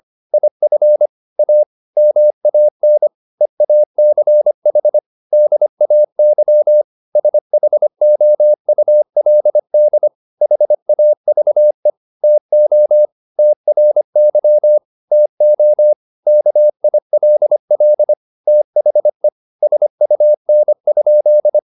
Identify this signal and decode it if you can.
Morse code, heard as a keyed tone, about 25 wpm, 600 Hz. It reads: IF A MAN EACH DAY SHOULD HAVE TO TRY TO KILL THE SUN?